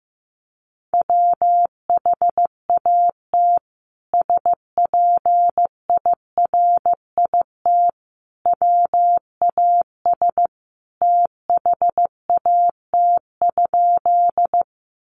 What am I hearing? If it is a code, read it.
Morse code, heard as a keyed tone, about 15 wpm, 700 Hz. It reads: WHAT SPIRIT WAS THAT?
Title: Sentences from Top 1400 Words 15wpm